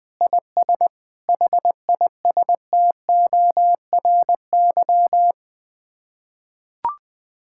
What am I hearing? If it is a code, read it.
Morse code, heard as a keyed tone, about 21 wpm, 700 Hz. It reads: IS HISTORY
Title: Sets of 8 Words - 20wpm